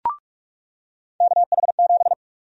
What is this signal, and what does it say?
Morse code, heard as a keyed tone, about 46 wpm, 700 Hz. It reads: KH7
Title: USA Call Sign Prefixes - 45wpm